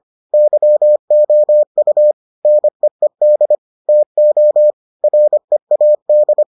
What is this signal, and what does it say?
Morse code, heard as a keyed tone, about 25 wpm, 600 Hz. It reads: YOU NEED TO READ